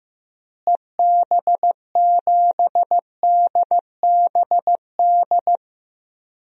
Morse code, heard as a keyed tone, about 15 wpm, 700 Hz. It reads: EB7DBD
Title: DXCC-100 Call Signs - 6 Characters - Repeated 3x - 15wpm